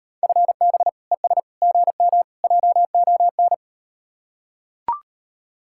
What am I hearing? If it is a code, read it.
Morse code, heard as a keyed tone, about 38 wpm, 700 Hz. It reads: FB ES GM JON